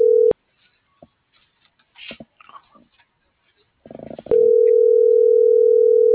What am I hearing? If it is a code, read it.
Ringback tone (North America), 440 + 480 Hz.
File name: CENTRAL GENERAL / out zabdi.escobedo